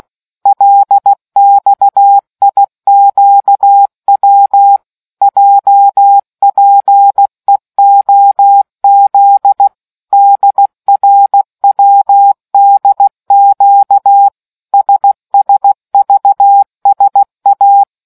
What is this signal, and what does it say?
Morse code, heard as a keyed tone, about 16 wpm, 800 Hz. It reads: LXIQW JPEOZ DRWDQ SSVSA